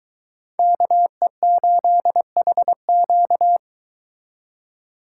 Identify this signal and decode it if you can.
Morse code, heard as a keyed tone, about 23 wpm, 700 Hz. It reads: KE8HQ